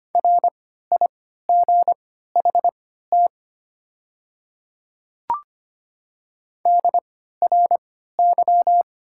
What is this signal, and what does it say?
Morse code, heard as a keyed tone, about 25 wpm, spaced as for 14 wpm, 700 Hz. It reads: RIGHT DRY
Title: 3x Character Spacing - Top 500 Words - 25wpm